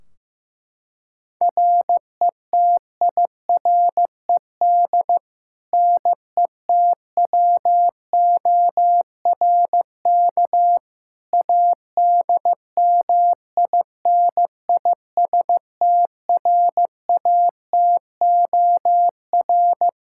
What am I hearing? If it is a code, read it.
Morse code, heard as a keyed tone, about 16 wpm, 700 Hz. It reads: RETIRED NETWORK ADMINISTRATOR
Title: K1USN SST QSO Practice 15wpm